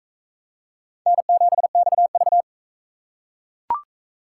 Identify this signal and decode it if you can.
Morse code, heard as a keyed tone, about 39 wpm, 700 Hz. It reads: N7XV